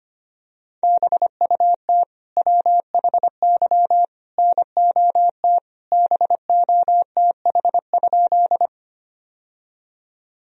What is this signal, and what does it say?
Morse code, heard as a keyed tone, about 25 wpm, 700 Hz. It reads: BUT WHY NOT BOTH?
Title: Sentences from Top 1000 Words - Review Part 1 - 25wpm